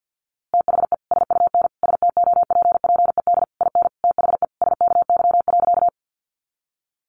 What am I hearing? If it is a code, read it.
Morse code, heard as a keyed tone, about 50 wpm, 700 Hz. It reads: THE SUN STOPPED IN THE SKY.
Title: A Sound Of Thunder 50wpm